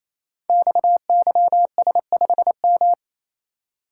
Morse code, heard as a keyed tone, about 28 wpm, 690 Hz. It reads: XYS5M
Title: Code Groups - 5 Letter:Numbers - 28wpm